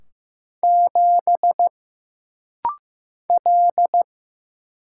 Morse code, heard as a keyed tone, about 15 wpm, 700 Hz. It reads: 7 L
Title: Single Letter-Number Alt - Rapid-Fire 15wpm